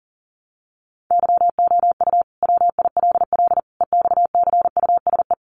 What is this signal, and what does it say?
Morse code, heard as a keyed tone, about 40 wpm, 700 Hz. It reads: YOU WILL EXCUSE
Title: Sentences from Top 700 Words 40wpm